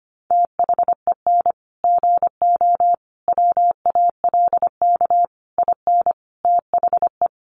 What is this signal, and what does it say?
Morse code, heard as a keyed tone, about 25 wpm, 700 Hz. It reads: THEN GO WALK IN THE
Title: Sentences from Top 1300 Words 25wpm